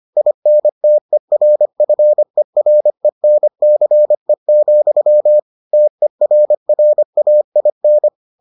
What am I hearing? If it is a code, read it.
Morse code, heard as a keyed tone, about 25 wpm, 600 Hz. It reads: INTERFERENCE, TERRAIN